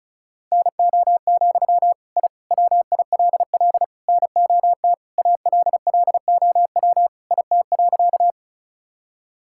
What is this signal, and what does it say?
Morse code, heard as a keyed tone, about 36 wpm, 700 Hz. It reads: NO, I WILL NOT ALLOW IT.
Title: Sentences from Top 1000 Words - Review Part 5 - 35wpm